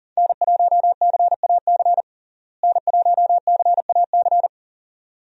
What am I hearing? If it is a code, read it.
Morse code, heard as a keyed tone, about 40 wpm, 700 Hz. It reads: N1CAC N1CAC
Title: USA Call Signs - 5 Characters - Repeated 2x - 40wpm